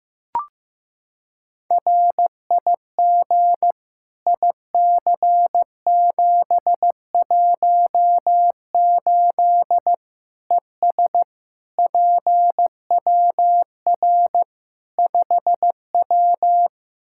Morse code, heard as a keyed tone, about 15 wpm, 700 Hz. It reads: RIG IC718 ES PWR 5W
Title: Practice QSO - Exchange 2 - Info - 15wpm